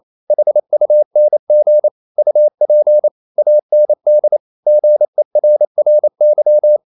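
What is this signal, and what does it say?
Morse code, heard as a keyed tone, about 28 wpm, 600 Hz. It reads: HUNG UP AND GERRY
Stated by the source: Morse code practice at 28 wpm, text of Ken Follett's novel Never in English, from CW med Gnister.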